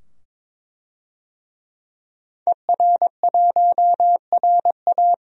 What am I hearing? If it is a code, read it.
Morse code, heard as a keyed tone, about 22 wpm, 700 Hz. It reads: ER1RA
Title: DXCC-100 Call Signs - 5 Characters - 22wpm